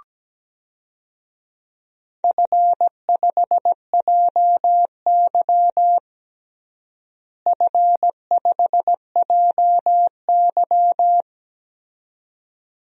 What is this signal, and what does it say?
Morse code, heard as a keyed tone, about 17 wpm, 700 Hz. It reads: F5JY F5JY
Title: DXCC-100 Call Signs - 4 Characters - Repeated 2x - 17wpm